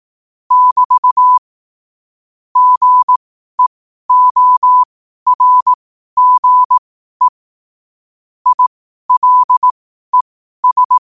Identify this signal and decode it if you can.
Morse code, heard as a keyed tone, about 18 wpm, spaced as for 12 wpm, 1000 Hz. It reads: <BT> GEORGE ILES
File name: qotdfiles / 12wpm